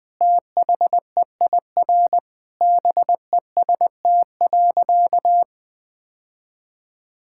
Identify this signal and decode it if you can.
Morse code, heard as a keyed tone, about 20 wpm, 700 Hz. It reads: THEIR BEST.
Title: Sentences with 2-Word Phrases - 20wpm